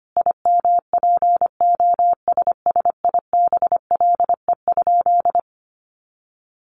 Morse code, heard as a keyed tone, about 25 wpm, 700 Hz. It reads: IMPOSSIBLE?
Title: Sentences from Top 1100 Words 25wpm